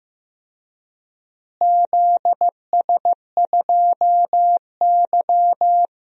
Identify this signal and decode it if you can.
Morse code, heard as a keyed tone, about 15 wpm, 700 Hz. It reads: ZS2Y
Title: DXCC-100 Call Signs - 4 Characters - Repeated 2x - 15wpm